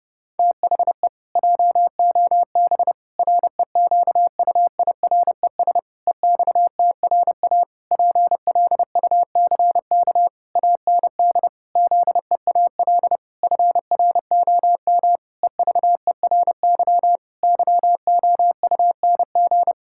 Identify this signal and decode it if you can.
Morse code, heard as a keyed tone, about 30 wpm, 700 Hz. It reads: THE JOB REQUIRES EXTRA PLUCK AND ZEAL FROM EVERY YOUNG
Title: Pangram Challenge v2 - 30wpm